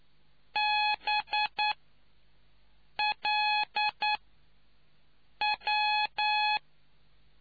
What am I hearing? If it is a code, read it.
Morse code, heard as a keyed tone, about 9 wpm, spaced as for 5 wpm, 840 Hz. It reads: BLW